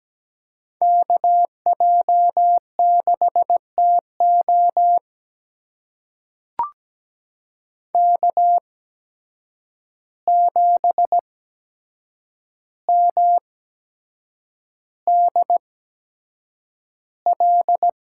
Morse code, heard as a keyed tone, about 17 wpm, spaced as for 7 wpm, 700 Hz. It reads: KJ6TO K7MDL
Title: USA Call Signs - 5 Characters - 8x Character Spacing - 17wpm